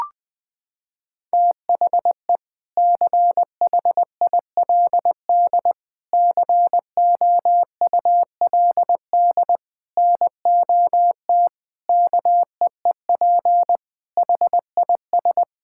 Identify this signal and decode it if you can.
Morse code, heard as a keyed tone, about 20 wpm, 700 Hz. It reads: THE CHILD COULD NOT KEEP HIS
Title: Sentences from Top 1000 Words - Review Part 6 - 20wpm